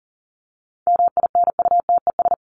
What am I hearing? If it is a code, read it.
Morse code, heard as a keyed tone, about 40 wpm, 700 Hz. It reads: MINUTES
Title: ES:S Suffix - Top 100 Words with Base Words 40wpm